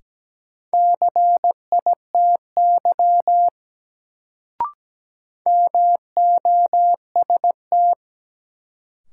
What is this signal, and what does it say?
Morse code, heard as a keyed tone, about 17 wpm, 700 Hz. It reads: CITY MOST
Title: Sets of 2 Words - New Word Drill - 17wpm